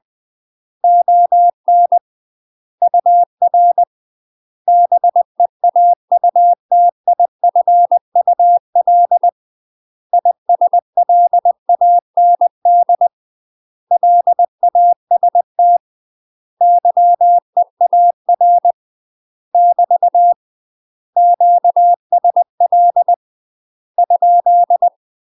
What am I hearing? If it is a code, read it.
Morse code, heard as a keyed tone, about 20 wpm, 700 Hz. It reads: ON UR BEAUTIFUL ISLAND LAST YEAR <BT> QSL ?
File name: UK MASTER QSOs TRAINING / 20WPM / UK MASTER QSOs 0011